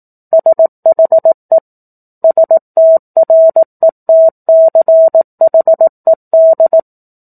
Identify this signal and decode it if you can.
Morse code, heard as a keyed tone, about 18 wpm, 650 Hz. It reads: SHE STRETCHED
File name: 04-Chapter-04 18 WPM